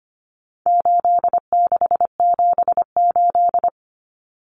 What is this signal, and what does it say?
Morse code, heard as a keyed tone, about 25 wpm, 700 Hz. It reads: 8678
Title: Numbers - 4-Digit 25wpm